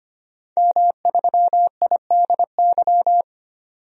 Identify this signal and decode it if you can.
Morse code, heard as a keyed tone, about 25 wpm, 700 Hz. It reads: M3IDY
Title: DXCC-100 Call Signs - 5 Characters - 25wpm